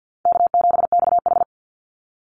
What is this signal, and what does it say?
Morse code, heard as a keyed tone, about 50 wpm, 700 Hz. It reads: K7XH
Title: Call Signs - 50wpm